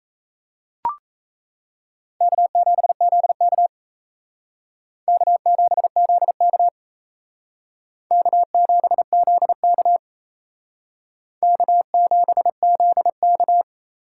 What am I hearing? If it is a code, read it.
Morse code, heard as a keyed tone, about 35 wpm, 700 Hz. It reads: K7ZK K7ZK K7ZK K7ZK
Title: Speed Racing - USA 1x2 Call Signs 28wpm